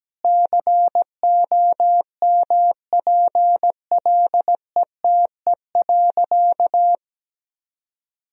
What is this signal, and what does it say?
Morse code, heard as a keyed tone, about 17 wpm, 700 Hz. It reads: COMPLETE.
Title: Sentences from Top 1000 Words - Review Part 2 - 17wpm